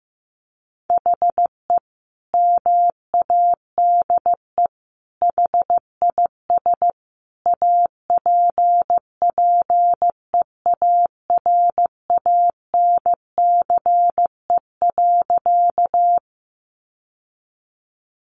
Morse code, heard as a keyed tone, about 15 wpm, 700 Hz. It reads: HE MADE HIS APPEARANCE.
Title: Sentences from Top 1100 Words 15wpm